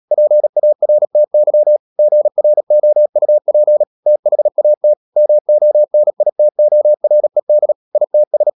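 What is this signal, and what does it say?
Morse code, heard as a keyed tone, about 37 wpm, 600 Hz. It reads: PARTY GROUP THAT MONITORED ITS
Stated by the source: Morse code practice at 35–39 wpm sending English text of Ken Follett's novel Never, from CW med Gnister.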